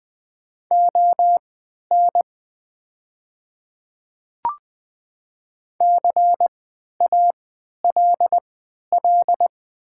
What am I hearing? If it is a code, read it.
Morse code, heard as a keyed tone, about 20 wpm, spaced as for 11 wpm, 700 Hz. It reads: ON CALL
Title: 3x Character Spacing - Top 100 Words - 20wpm